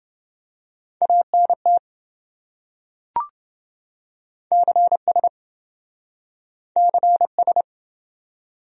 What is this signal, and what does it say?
Morse code, heard as a keyed tone, about 27 wpm, 700 Hz. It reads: ANT CS CS